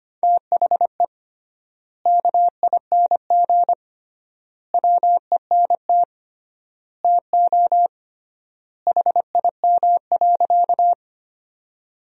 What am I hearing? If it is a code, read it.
Morse code, heard as a keyed tone, about 25 wpm, 700 Hz. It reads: THE KING WENT TO HIM.
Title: 3x Word-Spacing - Sentences from Top 400 Words - 25wpm